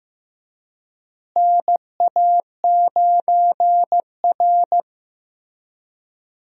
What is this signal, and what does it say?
Morse code, heard as a keyed tone, about 15 wpm, 700 Hz. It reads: NA9R
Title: USA Call Signs - 4 Characters - Repeated 2x - 15wpm